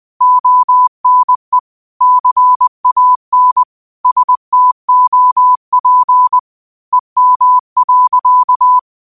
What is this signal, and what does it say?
Morse code, heard as a keyed tone, about 20 wpm, 1000 Hz. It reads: ONE CAN STOP EM.
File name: qotdfiles / 20wpm